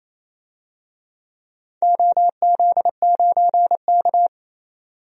Morse code, heard as a keyed tone, about 28 wpm, 700 Hz. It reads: OZ9K
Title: DXCC-100 Call Signs - 4 Characters - 28wpm